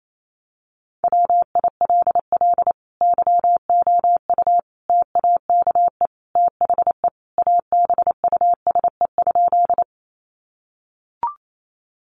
Morse code, heard as a keyed tone, about 28 wpm, 700 Hz. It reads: WILL YOU TAKE THE ABUSE?
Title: Sentences from Top 1300 Words 28wpm